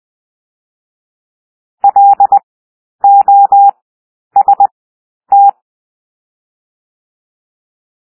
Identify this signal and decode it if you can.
Morse code, heard as a keyed tone, about 20 wpm, 800 Hz. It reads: L O S T
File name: cwmp3 / W1 SOUND WORDS 20 wpm